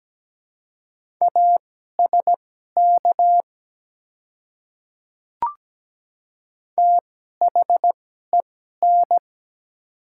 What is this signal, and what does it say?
Morse code, heard as a keyed tone, about 17 wpm, spaced as for 12 wpm, 700 Hz. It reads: ASK THEN
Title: 2x Character Spacing - Top 200 Words - 17wpm